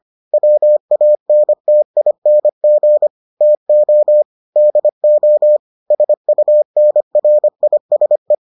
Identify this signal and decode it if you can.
Morse code, heard as a keyed tone, about 25 wpm, 600 Hz. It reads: WANTING TO DO SUNRISE